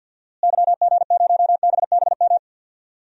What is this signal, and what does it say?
Morse code, heard as a keyed tone, about 50 wpm, 700 Hz. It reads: YG0BBM
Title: DXCC-100 Call Signs - 6 Characters - Repeated 3x - 50wpm